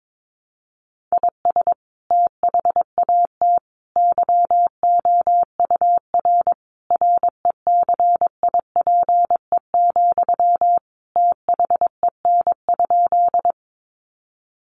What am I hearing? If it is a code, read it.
Morse code, heard as a keyed tone, about 22 wpm, 700 Hz. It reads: IS THAT YOUR RECIPE, THEN?
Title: Sentences from Top 1200 Words 22wpm